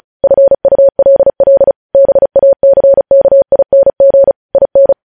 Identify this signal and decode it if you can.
Morse code, heard as a keyed tone, about 36 wpm, 550 Hz. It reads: FULL BACKING IN